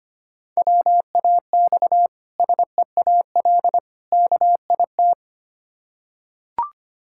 Morse code, heard as a keyed tone, about 25 wpm, 700 Hz. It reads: WAX SEAL KIT
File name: practice / 25wpm / Difficult Words with Hints - 25wpm